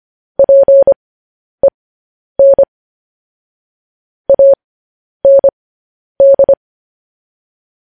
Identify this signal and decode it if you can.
Morse code, heard as a keyed tone, about 25 wpm, spaced as for 10 wpm, 550 Hz. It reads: PEN AND
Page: news-headlines-in-morse-code-at-10-wpm